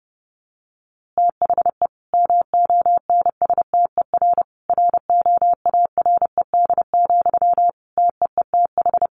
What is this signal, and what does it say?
Morse code, heard as a keyed tone, about 30 wpm, 700 Hz. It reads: THE MONSTER ROARED, TEETH
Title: A Sound Of Thunder 30wpm